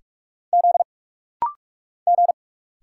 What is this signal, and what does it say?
Morse code, heard as a keyed tone, about 46 wpm, 700 Hz. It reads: Z G
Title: Single Letter-Number Alt - Rapid-Fire 45wpm